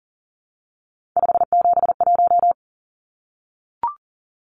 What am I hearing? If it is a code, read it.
Morse code, heard as a keyed tone, about 40 wpm, 700 Hz. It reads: 571
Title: Numbers - 3-Digit 40wpm